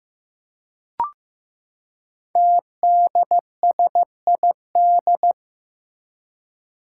Morse code, heard as a keyed tone, about 15 wpm, 700 Hz. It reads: TDSID
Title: Five Challenging Letter-Number Combinations 15wpm